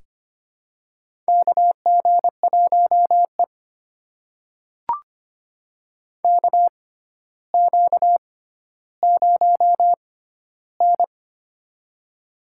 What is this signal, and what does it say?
Morse code, heard as a keyed tone, about 25 wpm, spaced as for 13 wpm, 700 Hz. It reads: KG1E KQ0N
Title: USA Call Signs - 4 Characters - 6x Character Spacing - 25wpm